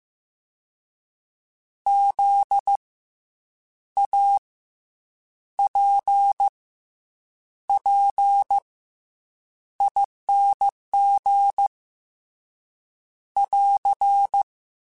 Morse code, heard as a keyed tone, about 15 wpm, 800 Hz. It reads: Z A P P ING <AR>
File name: Special INGs / Special INGs-mp3s / SlowToFaster ing @13